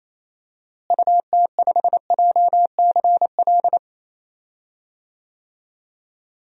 Morse code, heard as a keyed tone, about 28 wpm, 700 Hz. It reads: UT5JCL
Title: DXCC-100 Call Signs - 6 Characters - Repeated 2x - 28wpm